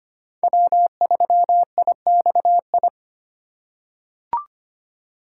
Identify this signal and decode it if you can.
Morse code, heard as a keyed tone, about 25 wpm, 700 Hz. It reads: W3IXI
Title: USA Call Signs - 5 Characters - 2x Character Spacing - 25wpm